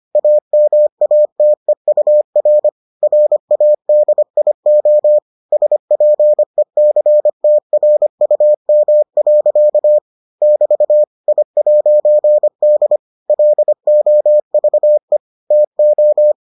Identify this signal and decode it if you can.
Morse code, heard as a keyed tone, about 25 wpm, 600 Hz. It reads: AMATEUR RADIO SPECTRUM. <BT> I'D LOVE TO